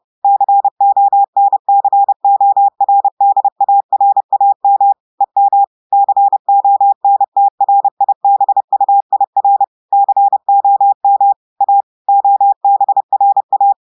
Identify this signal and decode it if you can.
Morse code, heard as a keyed tone, about 30 wpm, 800 Hz. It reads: CONCORDARAM EM CONTRIBUIR COM A OBRA
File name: audios / portugais / portugais texte en clair 1 vitesse 30 mots minutes